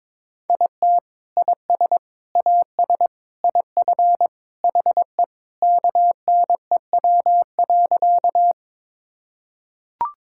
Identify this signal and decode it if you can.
Morse code, heard as a keyed tone, about 22 wpm, 700 Hz. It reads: IT IS AS IF HE KNEW.
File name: practice / 22wpm / Sentences with 3-Word Phrases - 22wpm